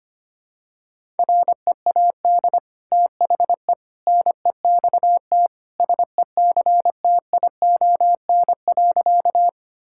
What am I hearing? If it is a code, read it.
Morse code, heard as a keyed tone, about 25 wpm, 700 Hz. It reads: READ THE NEXT SECTION.
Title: Sentences from Top 1000 Words - Review Part 5 - 25wpm